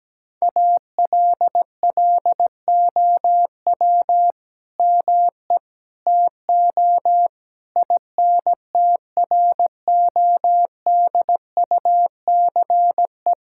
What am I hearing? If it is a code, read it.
Morse code, heard as a keyed tone, about 17 wpm, 700 Hz. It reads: ALLOW ME TO INTRODUCE
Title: Sentences from Top 1600 Words 17wpm